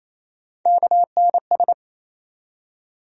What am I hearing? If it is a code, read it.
Morse code, heard as a keyed tone, about 28 wpm, 700 Hz. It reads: KNS
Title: USA Call Sign Postfixes - 3 Letters 28wpm